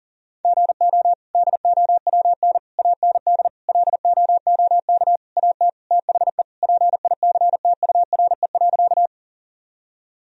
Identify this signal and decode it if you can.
Morse code, heard as a keyed tone, about 40 wpm, 700 Hz. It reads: GO DOWN AND LOOK AT THE PICTURE.
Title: Sentences from Top 1000 Words - Review Part 1 - 40wpm